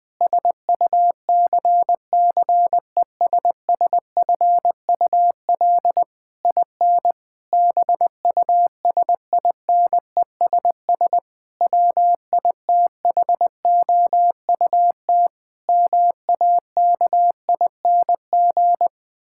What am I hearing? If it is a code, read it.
Morse code, heard as a keyed tone, about 20 wpm, 700 Hz. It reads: SUCCESSFUL IN BUSINESS WITHOUT MAKING
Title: Sentences from Top 1000 Words - Review Part 6 - 20wpm